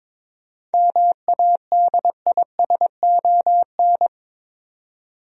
Morse code, heard as a keyed tone, about 22 wpm, 700 Hz. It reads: MADISON
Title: USA Common Names - 22wpm